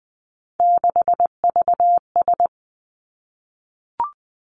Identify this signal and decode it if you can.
Morse code, heard as a keyed tone, about 20 wpm, 700 Hz. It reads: 6VS
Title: Three Challenging Letter-Number Combinations 20wpm